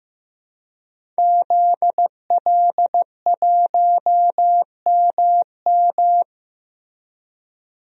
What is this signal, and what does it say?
Morse code, heard as a keyed tone, about 15 wpm, 700 Hz. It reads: ZL1MM
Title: DXCC-100 Call Signs - 5 Characters - Repeated 3x - 15wpm